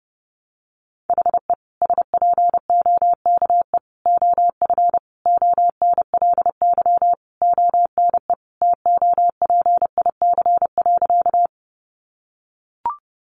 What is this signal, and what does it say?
Morse code, heard as a keyed tone, about 30 wpm, 700 Hz. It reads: HE SPOKE OF ONLY ONE TOPIC.
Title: Sentences from Top 1200 Words 30wpm